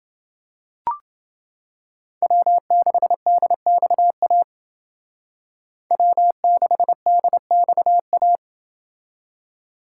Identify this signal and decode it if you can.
Morse code, heard as a keyed tone, about 28 wpm, 700 Hz. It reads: W6DXA W6DXA